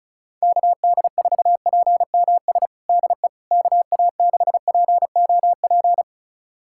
Morse code, heard as a keyed tone, about 36 wpm, 700 Hz. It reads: KD4PMS DE KA6POP
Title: USA Callsign DE Callsign 35wpm